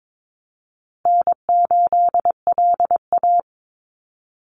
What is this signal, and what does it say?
Morse code, heard as a keyed tone, about 22 wpm, 700 Hz. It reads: N8LA
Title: Easy Call Signs Repeated 3x - 22wpm